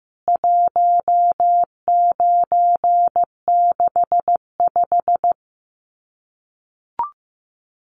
Morse code, heard as a keyed tone, about 15 wpm, 700 Hz. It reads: 1965
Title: CWT Exchange 15wpm